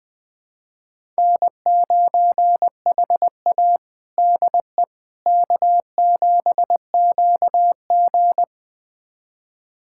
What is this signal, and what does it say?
Morse code, heard as a keyed tone, about 20 wpm, 700 Hz. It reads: N9HA DE K7QG
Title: USA Callsign DE Callsign - 4 Characters - 20wpm